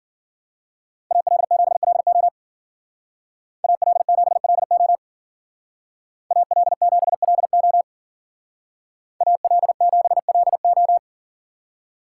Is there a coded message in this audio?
Yes: Morse code from 1.1 to 11.0 s, heard as a keyed tone, about 52 wpm, 700 Hz. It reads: AL7LO AL7LO AL7LO AL7LO